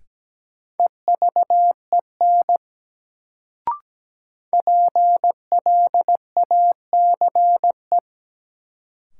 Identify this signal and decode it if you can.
Morse code, heard as a keyed tone, about 17 wpm, 700 Hz. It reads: EVEN PLACE
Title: Sets of 4 Words - New Word Drill - 17wpm